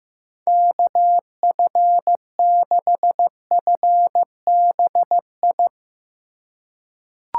Morse code, heard as a keyed tone, about 15 wpm, 700 Hz. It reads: KF6FBI
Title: USA Call Signs - 6 Characters - 2x Character Spacing - 15wpm